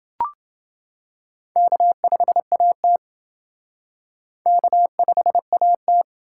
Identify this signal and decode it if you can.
Morse code, heard as a keyed tone, about 28 wpm, 700 Hz. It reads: K5AT K5AT